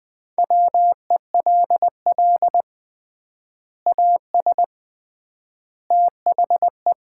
Morse code, heard as a keyed tone, about 20 wpm, 700 Hz. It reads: WELL AS THE
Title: 3x Word-Spacing - Sets of 3 Words - 20wpm